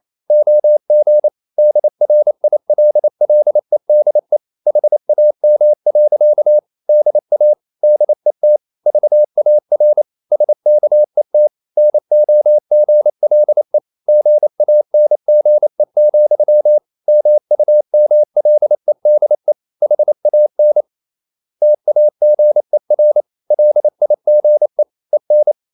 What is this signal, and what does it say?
Morse code, heard as a keyed tone, about 28 wpm, 600 Hz. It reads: OG DRILLEDE HAM. DA DET VAR SKET NOGLE GANGE, MUMLEDE HAN TAGER LIGE EN
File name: Ferie QRQ 28wpm